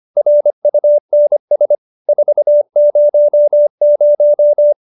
Morse code, heard as a keyed tone, about 25 wpm, 590 Hz. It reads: RUNS 400